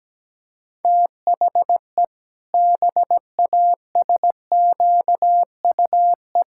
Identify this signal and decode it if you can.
Morse code, heard as a keyed tone, about 17 wpm, 700 Hz. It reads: THE BASQUE
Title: Pangram Challenge v2 - 17wpm